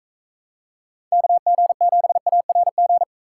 Morse code, heard as a keyed tone, about 40 wpm, 700 Hz. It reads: KG7ARG